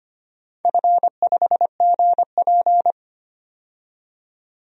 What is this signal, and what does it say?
Morse code, heard as a keyed tone, about 25 wpm, 700 Hz. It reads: F5GP
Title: DXCC-100 Call Signs - 4 Characters - 25wpm